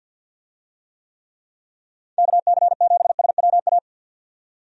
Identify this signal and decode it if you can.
Morse code, heard as a keyed tone, about 51 wpm, 700 Hz. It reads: KC7SWA
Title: USA Call Signs - 6 Characters - 50wpm